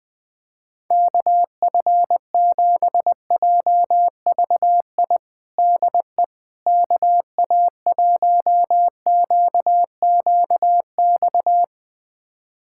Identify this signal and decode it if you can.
Morse code, heard as a keyed tone, about 20 wpm, 700 Hz. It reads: KF7JVI DE KA1QQX
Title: USA Callsign DE Callsign - 6 Characters - 20wpm